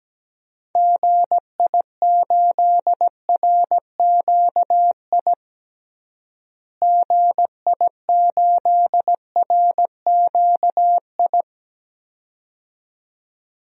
Morse code, heard as a keyed tone, about 17 wpm, 700 Hz. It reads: GI8RQI GI8RQI
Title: DXCC-100 Call Signs - 6 Characters - Repeated 2x - 17wpm